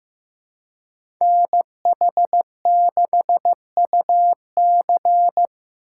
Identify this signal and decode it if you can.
Morse code, heard as a keyed tone, about 15 wpm, 700 Hz. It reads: NH6UC
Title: USA Call Signs - 5 Characters - 2x Character Spacing - 15wpm